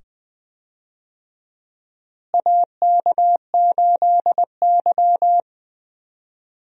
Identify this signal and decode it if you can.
Morse code, heard as a keyed tone, about 20 wpm, 700 Hz. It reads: AK8Y
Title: USA Call Signs - 4 Characters - Repeated 2x - 20wpm